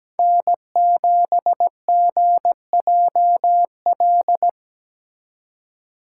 Morse code, heard as a keyed tone, about 17 wpm, 700 Hz. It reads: N7GJL
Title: USA Call Signs - 5 Characters - 2x Character Spacing - 17wpm